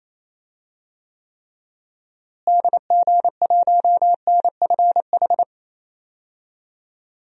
Morse code, heard as a keyed tone, about 28 wpm, 700 Hz. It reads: DG1NFH
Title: DXCC-100 Call Signs - 6 Characters - Repeated 3x - 28wpm